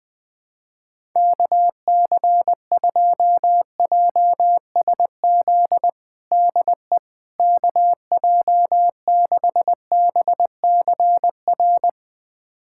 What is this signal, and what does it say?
Morse code, heard as a keyed tone, about 20 wpm, 700 Hz. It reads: KC2JSZ DE KJ6BCR
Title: USA Callsign DE Callsign - 6 Characters - 20wpm